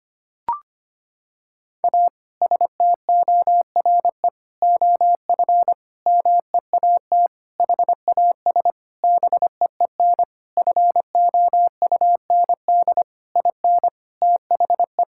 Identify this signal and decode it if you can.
Morse code, heard as a keyed tone, about 25 wpm, 700 Hz. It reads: A STORE OF MEAT HAS BEEN FOUND IN THE
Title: Sentences from Top 1000 Words - Review Part 5 - 25wpm